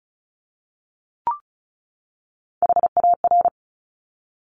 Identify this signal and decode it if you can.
Morse code, heard as a keyed tone, about 36 wpm, 700 Hz. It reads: HAR
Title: Top Three Letter Combinations in Words 35wpm